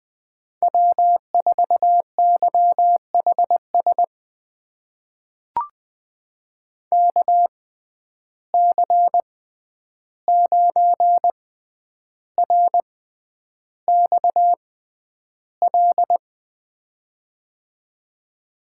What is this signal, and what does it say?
Morse code, heard as a keyed tone, about 20 wpm, spaced as for 10 wpm, 700 Hz. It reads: W4YHS KC9RXL
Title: USA Call Signs - 6x Character Spacing - 20wpm